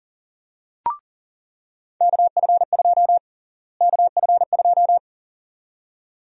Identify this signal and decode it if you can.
Morse code, heard as a keyed tone, about 40 wpm, 700 Hz. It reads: KF2 KF2
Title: USA Call Sign Prefixes - Repeated 2x - 40wpm